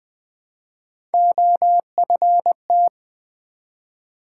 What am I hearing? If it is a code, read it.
Morse code, heard as a keyed tone, about 20 wpm, 700 Hz. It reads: OFT
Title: Build A Word - 1 Letter At A Time - 5-Letter Words - 20wpm